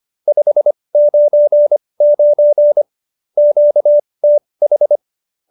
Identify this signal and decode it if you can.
Morse code, heard as a keyed tone, about 25 wpm, spaced as for 20 wpm, 590 Hz. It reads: 599 QTH
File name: qso216 20